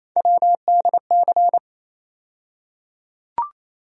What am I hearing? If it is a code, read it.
Morse code, heard as a keyed tone, about 28 wpm, 700 Hz. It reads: WDC
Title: USA Call Sign Postfixes - 3 Letters 28wpm